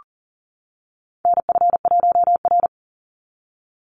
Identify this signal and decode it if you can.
Morse code, heard as a keyed tone, about 40 wpm, 700 Hz. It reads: NF1R
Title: NA SOTA Chaser Callsigns 40wpm